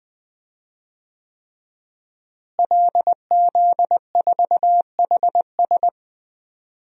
Morse code, heard as a keyed tone, about 20 wpm, 700 Hz. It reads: LZ4HS
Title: DXCC-100 Call Signs - 5 Characters - Repeated 3x - 20wpm